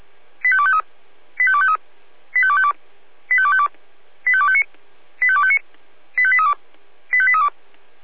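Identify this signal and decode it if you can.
Five-tone selective call (ZVEI), heard as quick runs of five tones, tones of about 70 ms. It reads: 85253 85253 85252 85252 85259 85259 85832 85832
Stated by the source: ZVEI five-tone selective call received by SDR.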